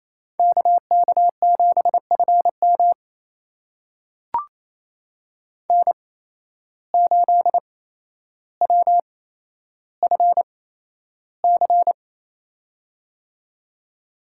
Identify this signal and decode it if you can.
Morse code, heard as a keyed tone, about 28 wpm, spaced as for 12 wpm, 700 Hz. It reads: KK7FM N8WFC
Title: USA Call Signs - 5 Characters - 8x Character Spacing - 28wpm